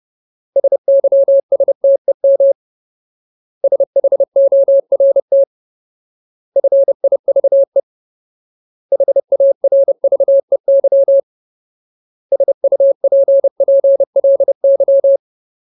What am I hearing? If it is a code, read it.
Morse code, heard as a keyed tone, about 30 wpm, 550 Hz. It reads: SYSTEM SHORT FIVE HARVEY SUPPLY
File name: Ultimate Word List / 30WPM Ultimate